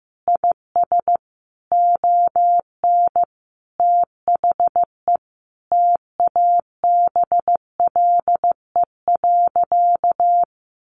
Morse code, heard as a keyed tone, about 15 wpm, 700 Hz. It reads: IS ON THE TABLE.
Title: Sentences from Top 1500 Words 15wpm